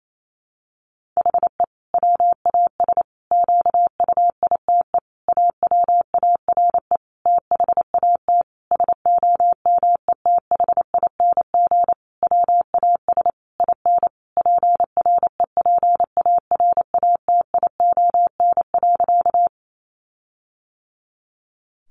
Morse code, heard as a keyed tone, about 28 wpm, 700 Hz. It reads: HE WAS QUITE AWARE THAT SOMETHING WAS IN PREPARATION.
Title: Sentences from Top 1400 Words 28wpm